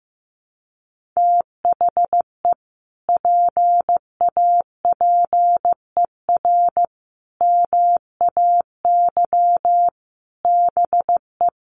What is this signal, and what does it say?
Morse code, heard as a keyed tone, about 15 wpm, 700 Hz. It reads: THE PAPER MAY BE
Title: Sentences from Top 1400 Words 15wpm